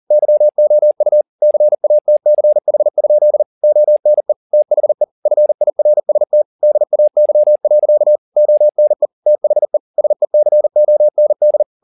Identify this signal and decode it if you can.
Morse code, heard as a keyed tone, about 40 wpm, 600 Hz. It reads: YOU CATCH? ONE THE FIRST DAY. ONE THE SECOND
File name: the old man and the sea 2 40wpm